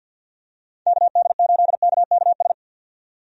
Morse code, heard as a keyed tone, about 51 wpm, 700 Hz. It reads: KD8KKS
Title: USA Call Signs - 6 Characters - 50wpm